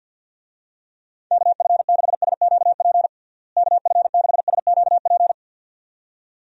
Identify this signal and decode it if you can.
Morse code, heard as a keyed tone, about 48 wpm, 700 Hz. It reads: KF6SQP KF6SQP
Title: USA Call Signs - 6 Characters - Repeated 2x - 50wpm